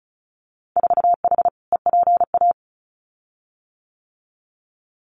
Morse code, heard as a keyed tone, about 36 wpm, 700 Hz. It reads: 4H EPA
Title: Winter Field Day Exchanges 35wpm